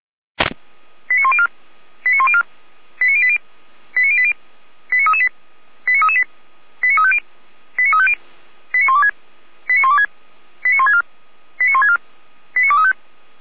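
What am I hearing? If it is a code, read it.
Five-tone selective call (ZVEI), heard as quick runs of five tones, tones of about 70 ms. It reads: 89104 89104 89080 89080 89338 89338 89350 89350 89126 89126 89164 89164 89235
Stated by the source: ZVEI five-tone call 89092 received on SDR.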